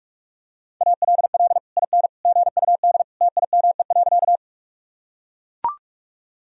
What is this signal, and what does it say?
Morse code, heard as a keyed tone, about 44 wpm, 700 Hz. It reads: ALL IN GUD TIME.
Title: Sentences from Top 200 Words and Abbreviations 45wpm